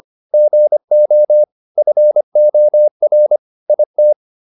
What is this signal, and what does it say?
Morse code, heard as a keyed tone, about 25 wpm, 600 Hz. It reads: GO FOR IT